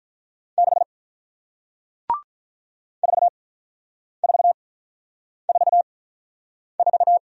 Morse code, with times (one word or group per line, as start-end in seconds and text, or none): 0.6-0.8 6
3.0-7.2 4444